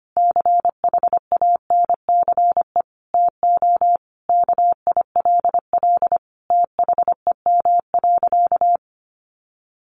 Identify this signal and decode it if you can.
Morse code, heard as a keyed tone, about 25 wpm, 700 Hz. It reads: CHANCE TO KILL THEM.
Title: Sentences from Top 1400 Words 25wpm